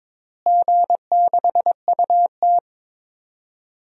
Morse code, heard as a keyed tone, about 22 wpm, 700 Hz. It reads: G6UT
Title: DXCC-100 Call Signs - 4 Characters - Repeated 3x - 22wpm